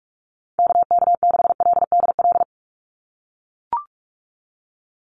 Morse code, heard as a keyed tone, about 44 wpm, 700 Hz. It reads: KK6LDL
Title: USA Call Signs 45wpm